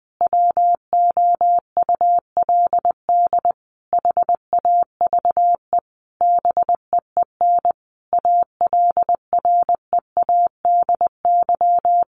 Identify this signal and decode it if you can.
Morse code, heard as a keyed tone, about 20 wpm, 700 Hz. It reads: WOULD HAVE BEEN ALREADY
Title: Sentences from Top 1500 Words 20wpm